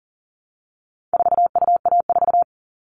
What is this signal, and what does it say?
Morse code, heard as a keyed tone, about 37 wpm, 700 Hz. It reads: 4UA4
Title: All Cut Numbers - 4-Digit 40wpm